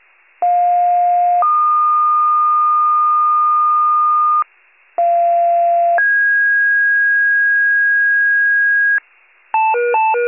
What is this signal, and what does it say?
Two-tone page, heard as a one-second tone, then a three-second tone, about 710 then 1220 Hz, about 690 then 1650 Hz, with noise. It is heard from 0.4 to 9.0 s.